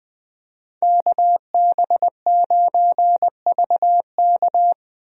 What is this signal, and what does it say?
Morse code, heard as a keyed tone, about 20 wpm, 700 Hz. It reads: KB9VK
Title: USA Call Signs - 5 Characters - 20wpm